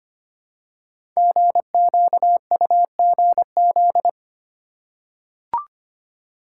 Morse code, heard as a keyed tone, about 25 wpm, 700 Hz. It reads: GQUGZ
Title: Code Groups - 5 Letters - 25wpm